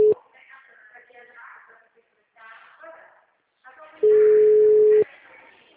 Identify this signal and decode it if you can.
Ringback tone (Europe, 425 Hz), with speech.
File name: out- -2045- - - .64319